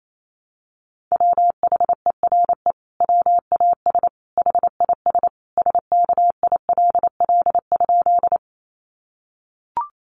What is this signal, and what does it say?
Morse code, heard as a keyed tone, about 28 wpm, 700 Hz. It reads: WHERE WAS HIS SKILL?
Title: Sentences from Top 1200 Words 28wpm